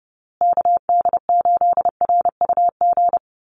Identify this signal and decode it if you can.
Morse code, heard as a keyed tone, about 30 wpm, 700 Hz. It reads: KD8RUG
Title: USA Call Signs Repeated 2x 30wpm